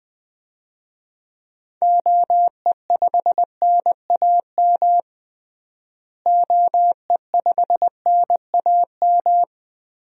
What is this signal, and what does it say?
Morse code, heard as a keyed tone, about 21 wpm, 700 Hz. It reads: OE5NAM OE5NAM
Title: DXCC-100 Call Signs - 6 Characters - Repeated 2x - 20wpm